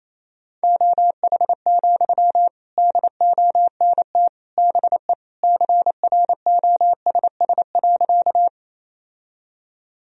Morse code, heard as a keyed tone, about 28 wpm, 700 Hz. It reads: OH, DONT BE CROSS.
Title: Sentences from Top 1000 Words - Review Part 1 - 28wpm